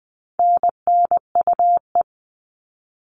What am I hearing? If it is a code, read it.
Morse code, heard as a keyed tone, about 20 wpm, 700 Hz. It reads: NNUE
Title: All Cut Numbers - 4-Digit 20wpm